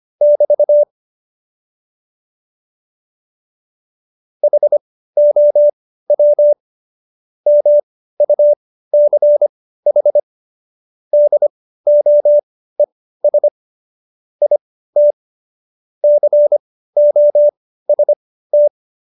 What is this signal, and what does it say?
Morse code, heard as a keyed tone, about 25 wpm, spaced as for 15 wpm, 590 Hz. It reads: <BT> HOW MUCH DOES IT COST
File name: Jokes-25-char-15-eff